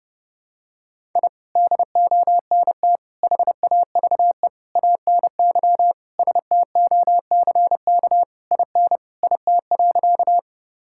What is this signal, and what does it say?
Morse code, heard as a keyed tone, about 30 wpm, 700 Hz. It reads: I DONT HAVE ANY STOCK IN IT.
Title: Sentences from Top 1000 Words - Review Part 5 - 30wpm